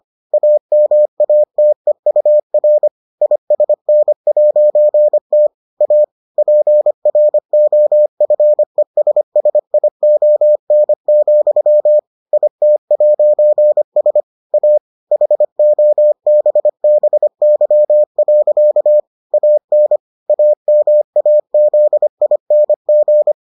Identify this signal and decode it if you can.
Morse code, heard as a keyed tone, about 25 wpm, 600 Hz. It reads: AMATEUR ISN'T A PROFESSION, IT'S A HOBBY. AN AMAZING